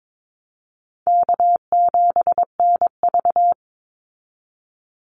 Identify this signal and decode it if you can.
Morse code, heard as a keyed tone, about 22 wpm, 700 Hz. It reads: K7NV
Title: Easy Call Signs Repeated 3x - 22wpm